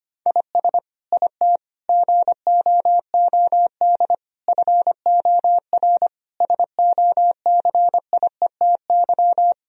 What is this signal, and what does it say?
Morse code, heard as a keyed tone, about 25 wpm, 700 Hz. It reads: IS IT GOOD FOR SOCIETY
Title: Sentences from Top 1000 Words - Review Part 4 - 25wpm